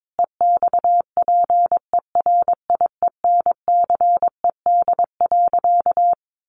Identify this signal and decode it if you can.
Morse code, heard as a keyed tone, about 22 wpm, 700 Hz. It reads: EXPERIENCED.
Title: Sentences from Top 1100 Words 22wpm